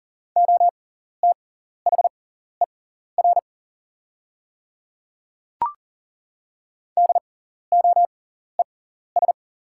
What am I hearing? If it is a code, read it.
Morse code, heard as a keyed tone, about 39 wpm, spaced as for 14 wpm, 690 Hz. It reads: OTHER DOES
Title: 6x Character Spacing - Top 500 Words - 40wpm